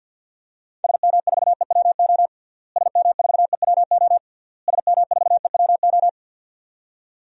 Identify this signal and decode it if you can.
Morse code, heard as a keyed tone, about 50 wpm, 700 Hz. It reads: SM4EWO SM4EWO SM4EWO
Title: DXCC-100 Call Signs - 6 Characters - Repeated 3x - 50wpm